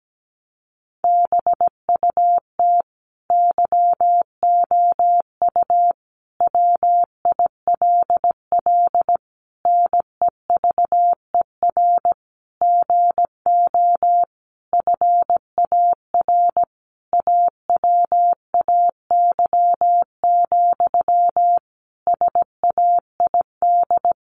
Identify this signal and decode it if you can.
Morse code, heard as a keyed tone, about 17 wpm, 700 Hz. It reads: BUT YOU WILL NEVER GO FAR AWAY, SAID
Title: Sentences from Top 1500 Words 17wpm